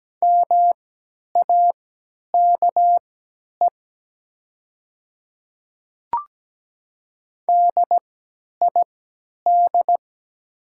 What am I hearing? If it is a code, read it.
Morse code, heard as a keyed tone, about 17 wpm, spaced as for 10 wpm, 700 Hz. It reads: MAKE DID
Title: 3x Character Spacing - Top 200 Words - 17wpm